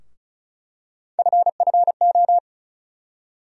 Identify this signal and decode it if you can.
Morse code, heard as a keyed tone, about 34 wpm, 700 Hz. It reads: FFO
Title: USA Call Sign Postfixes - 3 Letters 35wpm